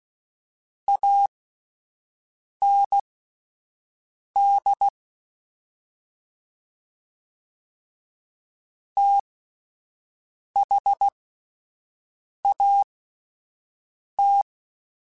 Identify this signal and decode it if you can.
Morse code, heard as a keyed tone, about 16 wpm, spaced as for 5 wpm, 800 Hz. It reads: AND THAT